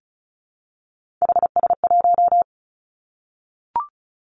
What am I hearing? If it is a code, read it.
Morse code, heard as a keyed tone, about 36 wpm, 700 Hz. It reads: HS1
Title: Three Challenging Letter-Number Combinations 35wpm